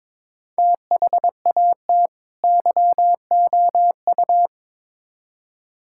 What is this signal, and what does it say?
Morse code, heard as a keyed tone, about 22 wpm, 700 Hz. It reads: THAT YOU
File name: Build A Phrase - 2-Word Phrases - 22wpm